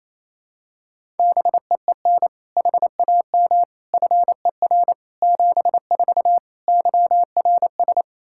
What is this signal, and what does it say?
Morse code, heard as a keyed tone, about 27 wpm, 700 Hz. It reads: BEEN HAM FER 74 YRS
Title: Practice QSO - Exchange 3 - More Info - 28wpm